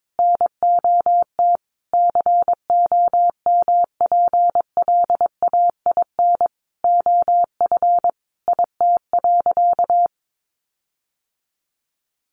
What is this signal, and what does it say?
Morse code, heard as a keyed tone, about 22 wpm, 700 Hz. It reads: NOT COMPLAIN OF IT.
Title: Sentences from Top 1300 Words 22wpm